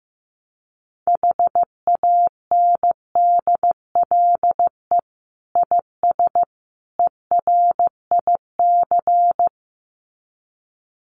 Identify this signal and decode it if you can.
Morse code, heard as a keyed tone, about 14 wpm, 700 Hz. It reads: HANDLE IS ERIC
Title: Common QSO Phrases 15wpm